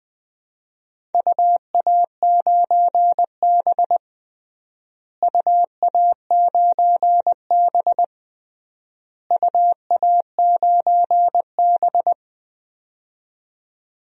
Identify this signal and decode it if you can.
Morse code, heard as a keyed tone, about 21 wpm, 700 Hz. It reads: UA9B UA9B UA9B
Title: DXCC-100 Call Signs - 4 Characters - Repeated 3x - 20wpm